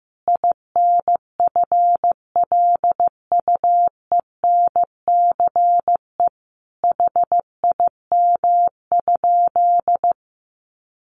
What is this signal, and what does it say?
Morse code, heard as a keyed tone, about 15 wpm, 700 Hz. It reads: INFLUENCE HIM?
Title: Sentences from Top 1100 Words 15wpm